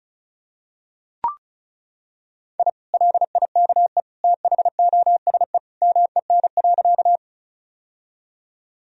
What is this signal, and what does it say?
Morse code, heard as a keyed tone, about 34 wpm, 700 Hz. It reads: I LIKE THOSE MEN.
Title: Sentences from Top 1000 Words - Review Part 1 - 35wpm